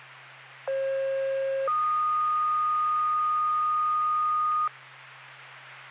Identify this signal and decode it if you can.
Two-tone page, heard as a one-second tone, then a three-second tone, about 540 then 1230 Hz. It is heard from 0.7 to 4.7 s.